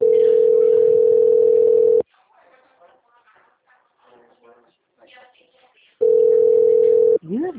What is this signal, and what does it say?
Ringback tone (North America), 440 + 480 Hz, with speech.
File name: out- -2007